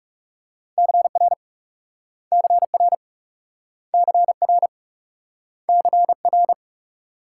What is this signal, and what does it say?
Morse code, heard as a keyed tone, about 42 wpm, 700 Hz. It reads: CR CR CR CR